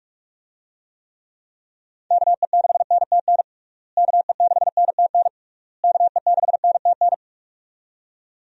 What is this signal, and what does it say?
Morse code, heard as a keyed tone, about 46 wpm, 700 Hz. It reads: KE6NTN KE6NTN KE6NTN
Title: USA Call Signs - 6 Characters - Repeated 3x - 45wpm